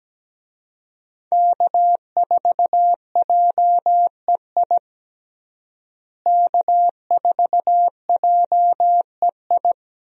Morse code, heard as a keyed tone, about 17 wpm, 700 Hz. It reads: K4JEI K4JEI
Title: USA Call Signs - 5 Characters - Repeated 2x - 17wpm